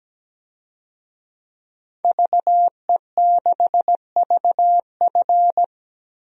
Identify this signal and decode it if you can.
Morse code, heard as a keyed tone, about 17 wpm, 700 Hz. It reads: VE6VF
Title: DXCC-100 Call Signs - 5 Characters - Repeated 2x - 17wpm